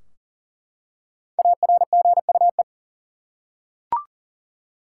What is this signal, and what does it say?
Morse code, heard as a keyed tone, about 39 wpm, 700 Hz. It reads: ARGUE
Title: Five Letter Words - 40wpm